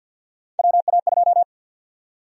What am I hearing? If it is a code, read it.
Morse code, heard as a keyed tone, about 51 wpm, 700 Hz. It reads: WA2
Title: USA Call Sign Prefixes - Repeated 2x - 50wpm